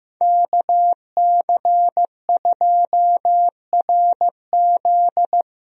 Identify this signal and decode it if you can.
Morse code, heard as a keyed tone, about 15 wpm, 700 Hz. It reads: KC2RZ
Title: USA Call Signs - 5 Characters - Repeated 2x - 15wpm